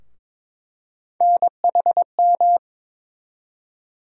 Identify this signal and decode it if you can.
Morse code, heard as a keyed tone, about 21 wpm, 700 Hz. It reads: NHM
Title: USA Call Sign Postfixes - 3 Letters 22wpm